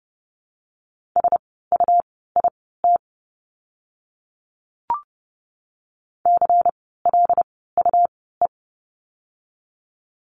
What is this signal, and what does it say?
Morse code, heard as a keyed tone, about 30 wpm, spaced as for 17 wpm, 700 Hz. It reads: SUIT CLUE
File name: Lesson 17f - Letter u - Words 30wpm